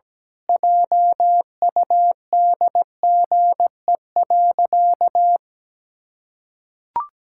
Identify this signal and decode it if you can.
Morse code, heard as a keyed tone, about 17 wpm, 700 Hz. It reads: JUDGE.
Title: Pangram Challenge v2 - 17wpm